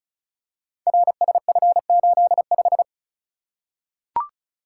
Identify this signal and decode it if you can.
Morse code, heard as a keyed tone, about 36 wpm, 700 Hz. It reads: RSF85
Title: Code Groups - 5 Letter:Numbers - 35wpm